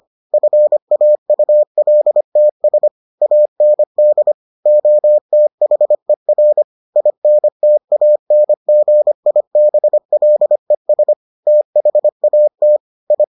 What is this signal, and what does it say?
Morse code, heard as a keyed tone, about 25 wpm, 600 Hz. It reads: FAULTS AND OTHER INTANGIBLES THAT I